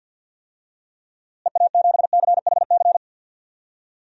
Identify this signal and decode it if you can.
Morse code, heard as a keyed tone, about 51 wpm, 700 Hz. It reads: EA7CHC
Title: DXCC-100 Call Signs - 6 Characters - Repeated 3x - 50wpm